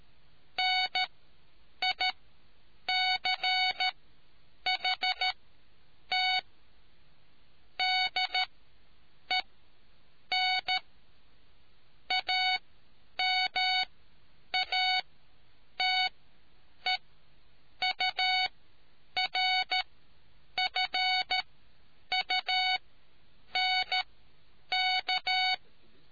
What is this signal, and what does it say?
Morse code, heard as a keyed tone, about 13 wpm, spaced as for 8 wpm, 740 Hz. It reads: NICHT DEN AMATEURFUNK